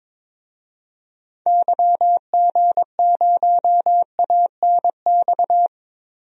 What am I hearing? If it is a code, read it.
Morse code, heard as a keyed tone, about 22 wpm, 700 Hz. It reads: YG0ANX
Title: DXCC-100 Call Signs - 6 Characters - 22wpm